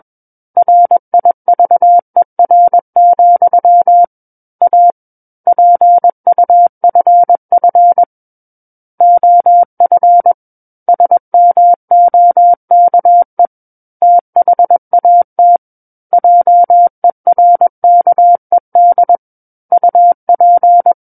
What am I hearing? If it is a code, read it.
Morse code, heard as a keyed tone, about 21 wpm, 700 Hz. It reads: RIVER, A PUFF OF SMOKE THAT JERKED UP
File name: Book One / 12-Chapter 12 - 21 WPM